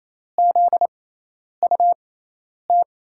Morse code, heard as a keyed tone, about 28 wpm, spaced as for 10 wpm, 700 Hz. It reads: ZUT
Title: USA Call Signs - 5 Characters - 6x Character Spacing - 28wpm